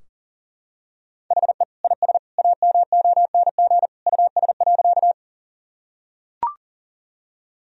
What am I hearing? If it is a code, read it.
Morse code, heard as a keyed tone, about 40 wpm, 700 Hz. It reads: HE IS AMONG US.
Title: Sentences from Top 1000 Words - Review Part 3 - 40wpm